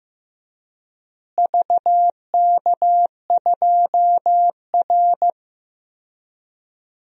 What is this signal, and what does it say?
Morse code, heard as a keyed tone, about 15 wpm, 700 Hz. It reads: VK2R
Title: DXCC-100 Call Signs - 4 Characters - Repeated 3x - 15wpm